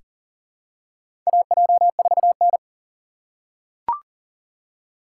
Morse code, heard as a keyed tone, about 37 wpm, 700 Hz. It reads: AJ4N
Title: USA Call Signs - 4x Character Spacing - 40wpm